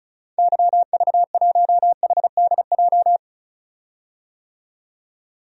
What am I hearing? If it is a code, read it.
Morse code, heard as a keyed tone, about 36 wpm, 700 Hz. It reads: YV1HDJ
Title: DXCC-100 Call Signs - 6 Characters - 35wpm